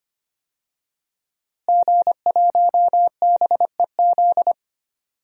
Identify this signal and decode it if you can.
Morse code, heard as a keyed tone, about 25 wpm, 700 Hz. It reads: G1BEZ
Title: DXCC-100 Call Signs - 5 Characters - Repeated 2x - 25wpm